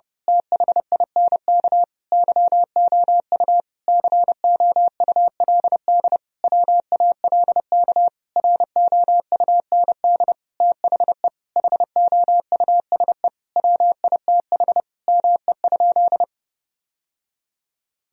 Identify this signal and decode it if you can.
Morse code, heard as a keyed tone, about 30 wpm, 700 Hz. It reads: THINK YOU COULD WALK ROUND THE HOUSE WITH ME?
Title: Sentences from Top 1000 Words - Review Part 1 - 30wpm